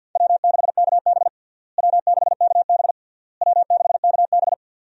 Morse code, heard as a keyed tone, about 51 wpm, 700 Hz. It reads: W6KB W6KB W6KB
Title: USA Call Signs - 4 Characters - Repeated 3x - 50wpm